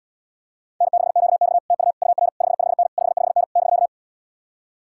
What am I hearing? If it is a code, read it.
Morse code, heard as a keyed tone, about 74 wpm, 700 Hz. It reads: N6QX TU UR 55N 55N <BK>